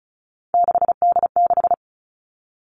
Morse code, heard as a keyed tone, about 36 wpm, 700 Hz. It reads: 6D6
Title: Three Challenging Letter-Number Combinations 35wpm